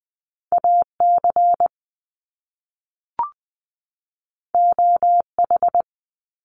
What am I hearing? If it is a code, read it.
Morse code, heard as a keyed tone, about 20 wpm, 700 Hz. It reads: AC OH
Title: Top Two Letter Combinations in Words 20wpm